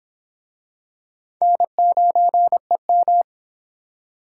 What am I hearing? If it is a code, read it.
Morse code, heard as a keyed tone, about 25 wpm, 700 Hz. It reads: N9EM